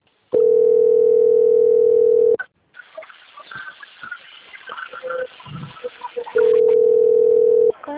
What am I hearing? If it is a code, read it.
Ringback tone (North America), 440 + 480 Hz.
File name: out- - - - - .3464